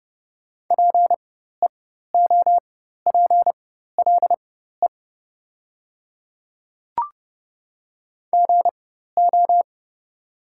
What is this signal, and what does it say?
Morse code, heard as a keyed tone, about 30 wpm, spaced as for 14 wpm, 700 Hz. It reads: PEOPLE GO
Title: 4x Character Spacing - Top 500 Words - 30wpm